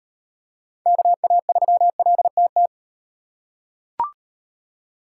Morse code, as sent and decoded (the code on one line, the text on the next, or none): -.- .- ...-- .-.. - -
KA3LTT